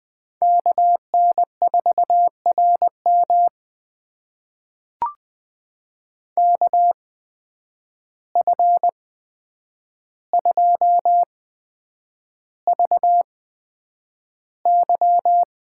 Morse code, heard as a keyed tone, about 20 wpm, spaced as for 9 wpm, 700 Hz. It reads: KN4RM KF2VY
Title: USA Call Signs - 8x Character Spacing - 20wpm